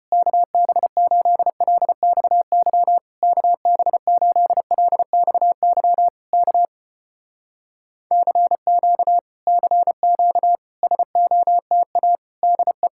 Morse code, heard as a keyed tone, about 34 wpm, 700 Hz. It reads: KB8LXY KB8LXY K CQ CQ SOTA DE